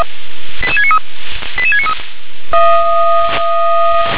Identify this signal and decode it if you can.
Five-tone selective call (ZVEI), heard as quick runs of five tones, tones of about 70 ms. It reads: 88693 88693